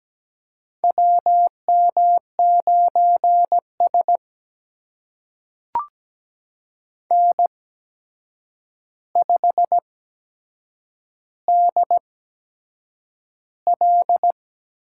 Morse code, heard as a keyed tone, about 17 wpm, spaced as for 7 wpm, 700 Hz. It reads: WM9S N5DL
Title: USA Call Signs - 4 Characters - 8x Character Spacing - 17wpm